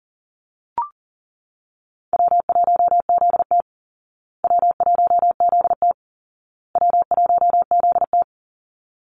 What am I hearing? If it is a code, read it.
Morse code, heard as a keyed tone, about 40 wpm, 700 Hz. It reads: W1ZT W1ZT W1ZT
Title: Call Signs Repeated 2x - 40wpm